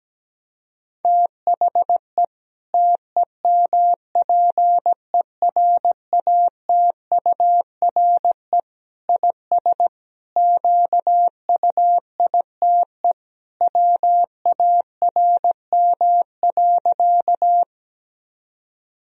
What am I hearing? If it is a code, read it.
Morse code, heard as a keyed tone, about 17 wpm, 700 Hz. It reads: THE TEMPERATURE IS QUITE WARM.
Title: Sentences from Top 1000 Words - Review Part 4 - 17wpm